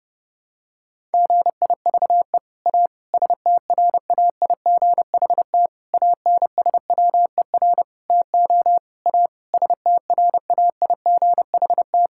Morse code, heard as a keyed tone, about 30 wpm, 700 Hz. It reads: GIVE A STRAIGHT ANSWER TO A STRAIGHT
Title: Sentences from Top 1000 Words - Review Part 6 - 30wpm